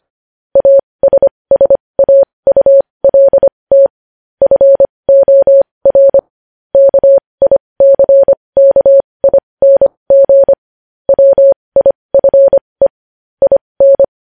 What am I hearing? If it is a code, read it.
Morse code, heard as a keyed tone, about 25 wpm, spaced as for 20 wpm, 550 Hz. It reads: ASSAULT FOR KICKING WIFE IN
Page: news-headlines-in-morse-code-at-20-wpm